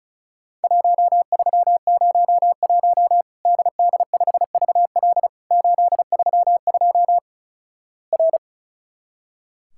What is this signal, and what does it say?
Morse code, heard as a keyed tone, about 36 wpm, 700 Hz. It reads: 1301 DD5VL 832 R
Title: Worked All Europe DX Contest - QTC Exchange - 35wpm